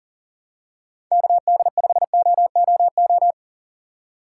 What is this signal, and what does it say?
Morse code, heard as a keyed tone, about 40 wpm, 700 Hz. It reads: KD5OOO
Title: USA Call Signs - 6 Characters - 2x Character Spacing - 40wpm